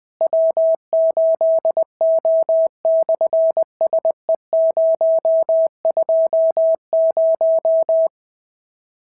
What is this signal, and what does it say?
Morse code, heard as a keyed tone, about 20 wpm, 650 Hz. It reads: W8O/SE020
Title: SOTA QSO Practice 20wpm